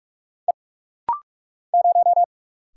Morse code, heard as a keyed tone, about 44 wpm, 700 Hz. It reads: E 0
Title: Single Letter-Number - Rapid-Fire 45wpm